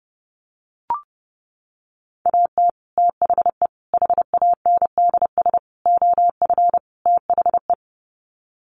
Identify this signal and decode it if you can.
Morse code, heard as a keyed tone, about 30 wpm, 700 Hz. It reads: AT THE HANDS OF THE
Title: Sets of 5 Words Encore2 30wpm